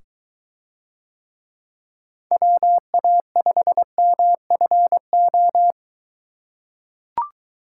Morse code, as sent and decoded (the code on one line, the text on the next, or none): .-- .- ..... -- ..-. ---
WA5MFO